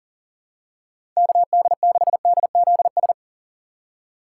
Morse code, heard as a keyed tone, about 40 wpm, 700 Hz. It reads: KD6DZS
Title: USA Call Signs - 6 Characters - 6x Character Spacing - 40wpm